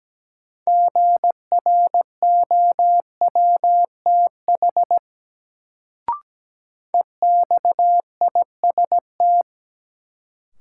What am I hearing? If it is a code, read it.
Morse code, heard as a keyed tone, about 17 wpm, 700 Hz. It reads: GROWTH EXIST
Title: Sentences from Top 1000 Words - Review Part 5 - New Word Drill - 17wpm